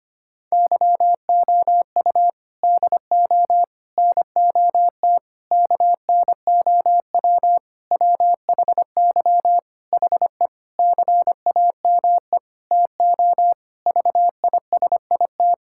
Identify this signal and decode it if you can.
Morse code, heard as a keyed tone, about 25 wpm, 700 Hz. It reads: YOU DO NOT KNOW WHY HE CAME TO VISIT
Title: Sentences from Top 1000 Words - Review Part 6 - 25wpm